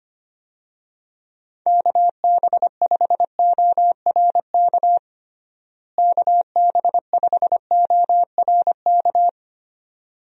Morse code, heard as a keyed tone, about 25 wpm, 700 Hz. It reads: KB5ORK KB5ORK
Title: USA Call Signs - 6 Characters - Repeated 2x - 25wpm